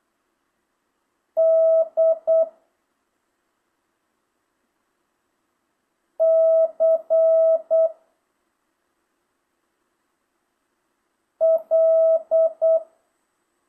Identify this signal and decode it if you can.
Morse code, heard as a keyed tone, about 8 wpm, 650 Hz. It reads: D C L
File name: jh8cbh / 2025cw